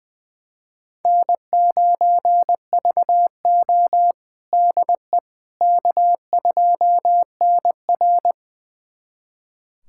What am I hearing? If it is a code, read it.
Morse code, heard as a keyed tone, about 20 wpm, 700 Hz. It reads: N9VO DE K2NR
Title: USA Callsign DE Callsign - 4 Characters - 20wpm